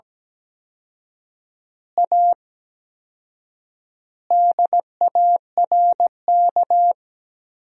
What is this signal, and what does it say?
Morse code, heard as a keyed tone, about 17 wpm, 700 Hz. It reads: A DARK
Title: 4x Word-Spacing - Sentences from Top 500 Words - 17wpm